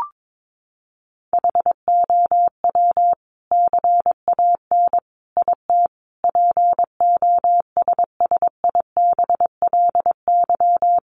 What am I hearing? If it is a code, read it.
Morse code, heard as a keyed tone, about 22 wpm, 700 Hz. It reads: HOW CAN IT POSSIBLY
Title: Sentences from Top 1300 Words 22wpm